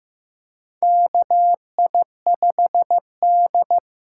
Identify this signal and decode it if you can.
Morse code, heard as a keyed tone, about 15 wpm, 700 Hz. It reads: KI5D
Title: USA Call Signs - 6 Characters - 6x Character Spacing - 15wpm